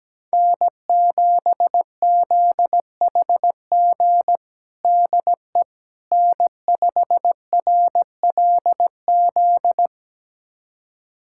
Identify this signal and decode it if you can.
Morse code, heard as a keyed tone, about 17 wpm, 700 Hz. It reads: N7ZHG DE N5RLZ
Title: USA Callsign DE Callsign - 5 Characters - 17wpm